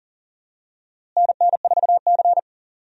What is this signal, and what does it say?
Morse code, heard as a keyed tone, about 40 wpm, 700 Hz. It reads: NN4C
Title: USA Call Signs - 4 Characters - Repeated 3x - 40wpm